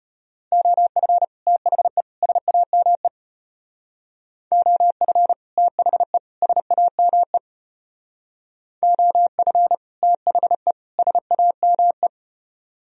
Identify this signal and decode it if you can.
Morse code, heard as a keyed tone, about 32 wpm, 700 Hz. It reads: OF THE SAME OF THE SAME OF THE SAME